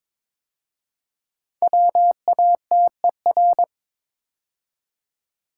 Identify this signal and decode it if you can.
Morse code, heard as a keyed tone, about 22 wpm, 690 Hz. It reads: WATER
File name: 5 Letter Words - First 3 Letters Given - 22wpm